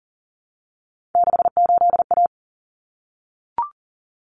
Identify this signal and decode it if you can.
Morse code, heard as a keyed tone, about 40 wpm, 700 Hz. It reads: 68A
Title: Common Cut Numbers - 3-Digit 40wpm